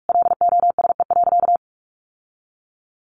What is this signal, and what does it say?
Morse code, heard as a keyed tone, about 44 wpm, 700 Hz. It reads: LOSE.
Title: Sentences from Top 1100 Words 45wpm